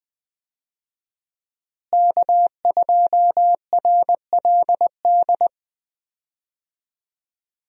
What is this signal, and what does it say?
Morse code, heard as a keyed tone, about 20 wpm, 700 Hz. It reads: K2RLD
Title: USA Call Signs - 5 Characters - Repeated 2x - 20wpm